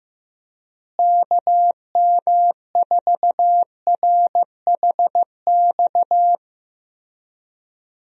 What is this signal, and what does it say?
Morse code, heard as a keyed tone, about 15 wpm, 700 Hz. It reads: KM4RHX
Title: USA Call Signs - 6 Characters - Repeated 3x - 15wpm